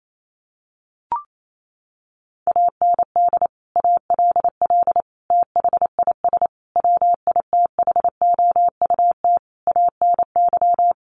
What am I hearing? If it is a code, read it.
Morse code, heard as a keyed tone, about 28 wpm, 700 Hz. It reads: AND ALL THIS WITHOUT ANY
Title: Sentences from Top 1200 Words 28wpm